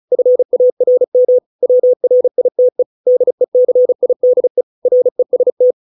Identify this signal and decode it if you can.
Morse code, heard as a keyed tone, about 34 wpm, 500 Hz. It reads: FARM WRITE DECIDE REST